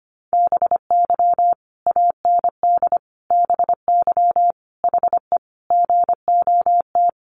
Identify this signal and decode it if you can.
Morse code, heard as a keyed tone, about 25 wpm, 700 Hz. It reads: BY AND BY HE GOT
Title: Sentences from Top 300 Words 25wpm